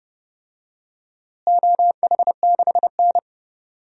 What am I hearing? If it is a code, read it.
Morse code, heard as a keyed tone, about 30 wpm, 700 Hz. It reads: OH6N
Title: DXCC-100 Call Signs - 4 Characters - Repeated 3x - 30wpm